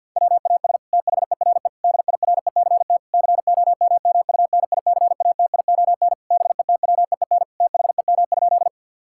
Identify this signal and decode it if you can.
Morse code, heard as a keyed tone, about 47 wpm, 700 Hz. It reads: WAS THERE DIRECT COMMUNICATION BETWEEN THEM?
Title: Sentences from Top 1000 Words - Review Part 6 - 50wpm